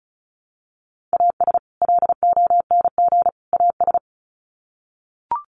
Morse code, heard as a keyed tone, about 34 wpm, 700 Hz. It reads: AS LONG AS
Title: Sets of 3 Words 35wpm